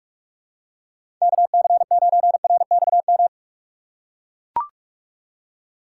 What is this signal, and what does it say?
Morse code, heard as a keyed tone, about 41 wpm, 700 Hz. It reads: KC9RXM
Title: USA Call Signs - 6 Characters - 3x Character Spacing - 45wpm